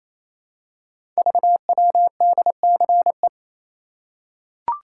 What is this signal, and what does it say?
Morse code, heard as a keyed tone, about 27 wpm, 700 Hz. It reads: VWDCE
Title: Code Groups - 5 Letters - 28wpm